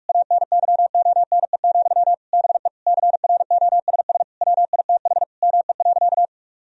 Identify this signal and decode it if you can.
Morse code, heard as a keyed tone, about 46 wpm, 700 Hz. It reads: ANYONE, BE CROSS WITH ME.
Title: Sentences from Top 1000 Words - Review Part 1 - 45wpm